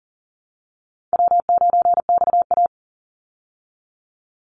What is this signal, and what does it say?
Morse code, heard as a keyed tone, about 40 wpm, 700 Hz. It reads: W9XA
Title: Call Signs Repeated 3x - 40wpm